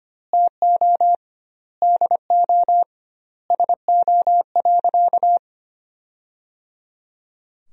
Morse code, heard as a keyed tone, about 25 wpm, 700 Hz. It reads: TO DO SO.
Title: 2x Word-Spacing - Sentences from Top 500 Words - 25wpm